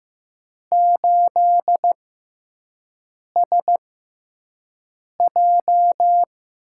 Morse code, heard as a keyed tone, about 15 wpm, 700 Hz. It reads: 8 S J
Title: USA Call Signs - 6x Character Spacing - 15wpm